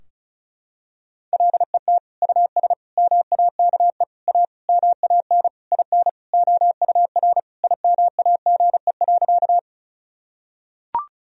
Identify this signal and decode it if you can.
Morse code, heard as a keyed tone, about 36 wpm, 700 Hz. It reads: LET US MAKE A MAN IN OUR IMAGE.
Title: Sentences from Top 1000 Words - Review Part 4 - 35wpm